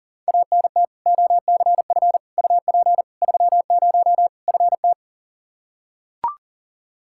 Morse code, heard as a keyed tone, about 40 wpm, 700 Hz. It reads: ANT OCF UP 30 FT
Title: Practice QSO - Exchange 2 - Info - 40wpm